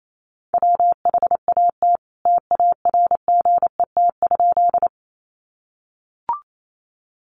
Morse code, heard as a keyed tone, about 28 wpm, 700 Hz. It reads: WHAT TARGET?
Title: Sentences from Top 1200 Words 28wpm